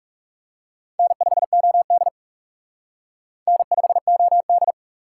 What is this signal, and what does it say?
Morse code, heard as a keyed tone, about 37 wpm, 700 Hz. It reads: N5OD N5OD
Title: Speed Racing - USA Call Signs 30wpm